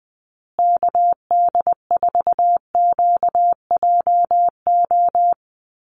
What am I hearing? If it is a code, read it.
Morse code, heard as a keyed tone, about 20 wpm, 700 Hz. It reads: KD4QJO
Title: USA Call Signs Repeated 2x 20wpm